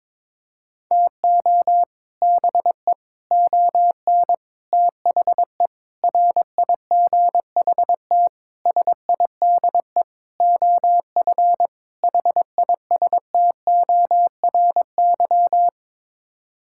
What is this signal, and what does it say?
Morse code, heard as a keyed tone, about 21 wpm, 700 Hz. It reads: TO BE ON THE RIGHT SIDE OF HISTORY
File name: practice / 22wpm / Sets of 8 Words - 22wpm